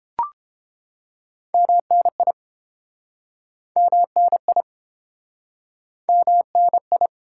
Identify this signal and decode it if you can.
Morse code, heard as a keyed tone, about 32 wpm, 700 Hz. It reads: MNI MNI MNI